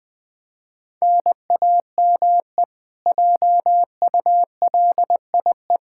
Morse code, heard as a keyed tone, about 20 wpm, 700 Hz. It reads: NAME JULIE
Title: Practice QSO - Exchange 1 - Beginning - 20wpm